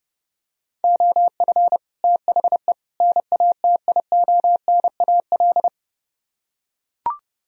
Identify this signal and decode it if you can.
Morse code, heard as a keyed tone, about 31 wpm, 700 Hz. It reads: OF THE NATIONAL